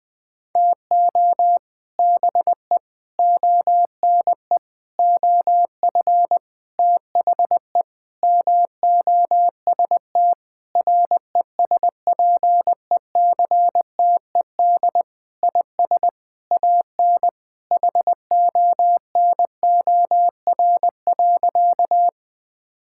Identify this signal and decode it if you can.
Morse code, heard as a keyed tone, about 20 wpm, 700 Hz. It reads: TO BE ONE OF THE MOST RESPECTED IS AN HONOR.
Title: Sentences with 6-Word Phrases - Spoken Beginning - 20wpm